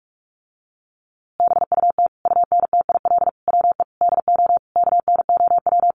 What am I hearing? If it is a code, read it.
Morse code, heard as a keyed tone, about 44 wpm, 700 Hz. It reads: BUT UNTIL WE DO KNOW
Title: A Sound Of Thunder 45wpm